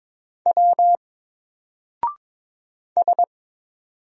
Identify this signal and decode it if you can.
Morse code, heard as a keyed tone, about 22 wpm, 700 Hz. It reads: W S
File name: Single-Letter-Number-22wpm